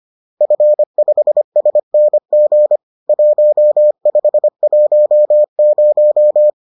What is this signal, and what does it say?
Morse code, heard as a keyed tone, about 25 wpm, 600 Hz. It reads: F5SNG 1510
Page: bulletin UFT sous forme texte et en CW